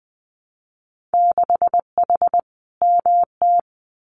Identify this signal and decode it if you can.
Morse code, heard as a keyed tone, about 20 wpm, 700 Hz. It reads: 6H MT
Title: Winter Field Day Exchanges Encore 20wpm